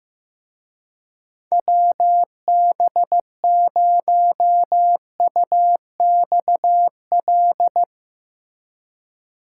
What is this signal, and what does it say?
Morse code, heard as a keyed tone, about 17 wpm, 700 Hz. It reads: WB0UXL
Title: Speed Racing - USA 2x3 Call Signs 15wpm